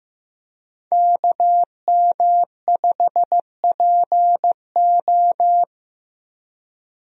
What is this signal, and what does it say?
Morse code, heard as a keyed tone, about 15 wpm, 700 Hz. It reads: KM5PO
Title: USA Call Signs - 5 Characters - Repeated 2x - 15wpm